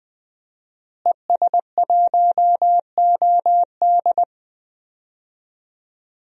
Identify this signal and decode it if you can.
Morse code, heard as a keyed tone, about 20 wpm, 700 Hz. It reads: ES1OD
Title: DXCC-100 Call Signs - 5 Characters - Repeated 2x - 20wpm